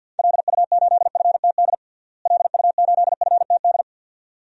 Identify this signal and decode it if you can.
Morse code, heard as a keyed tone, about 53 wpm, 700 Hz. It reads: LU8FTD LU8FTD
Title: DXCC-100 Call Signs - 6 Characters - Repeated 2x - 50wpm